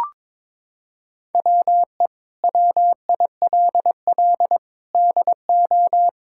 Morse code, heard as a keyed tone, about 21 wpm, 700 Hz. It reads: WE WILL DO
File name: practice / 22wpm / Sets of 8 Words - 22wpm